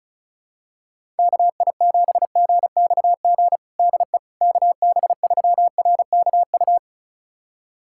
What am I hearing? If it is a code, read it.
Morse code, heard as a keyed tone, about 34 wpm, 700 Hz. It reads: KI7GXG DE KB3RKU
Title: USA Callsign DE Callsign 35wpm